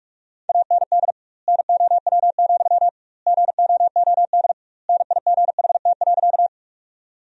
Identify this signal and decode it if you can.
Morse code, heard as a keyed tone, about 46 wpm, 700 Hz. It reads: AND NOW, GOOD NIGHT.
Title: 2x Word-Spacing - Sentences from Top 300 Words - 45wpm